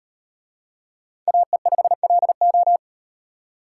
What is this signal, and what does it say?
Morse code, heard as a keyed tone, about 37 wpm, 700 Hz. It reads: AE5LO